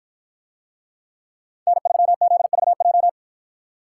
Morse code, heard as a keyed tone, about 49 wpm, 700 Hz. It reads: N3ZVJ